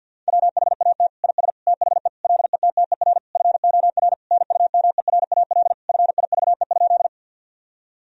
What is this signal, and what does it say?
Morse code, heard as a keyed tone, about 50 wpm, 700 Hz. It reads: WHAT IS THE LETTER FOR NUMERAL FIVE?
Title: Sentences from Top 1000 Words - Review Part 2 - 50wpm